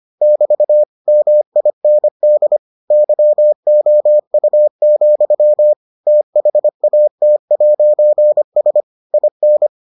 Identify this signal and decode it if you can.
Morse code, heard as a keyed tone, about 25 wpm, 600 Hz. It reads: <BT> MIND YOU, THAT'S IN